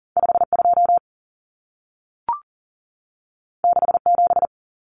Morse code, heard as a keyed tone, about 40 wpm, 700 Hz. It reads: 52 67
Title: Common Cut Numbers - 2-Digit 40wpm